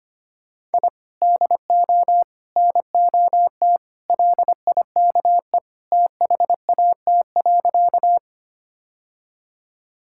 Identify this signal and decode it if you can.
Morse code, heard as a keyed tone, about 25 wpm, 700 Hz. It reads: I DO NOT LIKE THAT.
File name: practice / 25wpm / Sentences with 3-Word Phrases - 25wpm